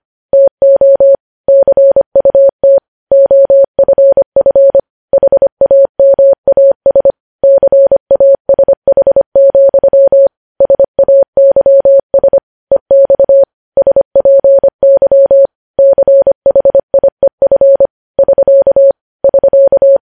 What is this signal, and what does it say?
Morse code, heard as a keyed tone, about 25 wpm, 550 Hz. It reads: TO CUT OFF HAMAS CASH, SAYS EX SPY CHIEF <SK> <SK>
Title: News Headlines at 25